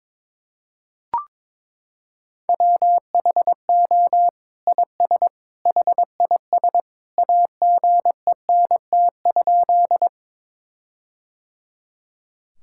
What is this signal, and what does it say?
Morse code, heard as a keyed tone, about 21 wpm, 700 Hz. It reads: WHO IS HIS AGENT?
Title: Sentences from Top 1600 Words 22wpm